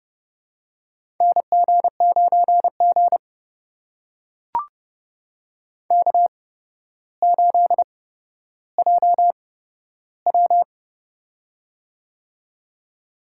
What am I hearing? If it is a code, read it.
Morse code, heard as a keyed tone, about 29 wpm, spaced as for 13 wpm, 700 Hz. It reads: NG9G K8JW
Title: USA Call Signs - 4 Characters - 8x Character Spacing - 30wpm